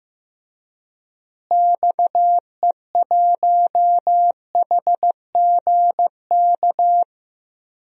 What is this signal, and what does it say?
Morse code, heard as a keyed tone, about 15 wpm, 700 Hz. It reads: XE1HGK
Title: DXCC-100 Call Signs - 6 Characters - Repeated 3x - 15wpm